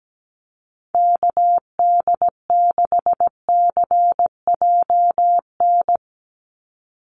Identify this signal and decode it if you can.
Morse code, heard as a keyed tone, about 17 wpm, 700 Hz. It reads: KD6CJN
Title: USA Call Signs 17wpm